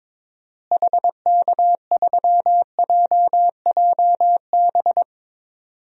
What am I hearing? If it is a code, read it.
Morse code, heard as a keyed tone, about 22 wpm, 700 Hz. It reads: HK3JJB
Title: DXCC-100 Call Signs - 6 Characters - Repeated 2x - 22wpm